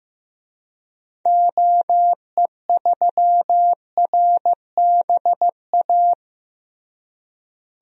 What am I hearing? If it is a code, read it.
Morse code, heard as a keyed tone, about 15 wpm, 700 Hz. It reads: OE3RBA
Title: DXCC-100 Call Signs - 6 Characters - Repeated 3x - 15wpm